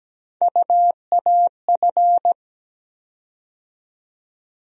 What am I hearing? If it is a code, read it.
Morse code, heard as a keyed tone, about 17 wpm, 700 Hz. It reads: UAF
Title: USA Call Sign Postfixes - 3 Letters 17wpm